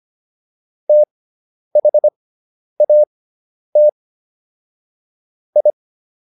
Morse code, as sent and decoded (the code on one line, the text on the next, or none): - .... .- - / ..
THAT I